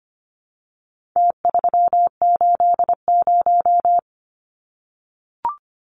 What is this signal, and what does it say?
Morse code, heard as a keyed tone, about 25 wpm, 700 Hz. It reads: T380
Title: Numbers - Contest Style - 4-Digit 25wpm